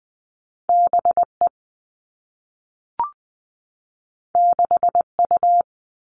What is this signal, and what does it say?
Morse code, heard as a keyed tone, about 20 wpm, 700 Hz. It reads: B E 6 U
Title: Two Challenging Letter-Number Combinations 20wpm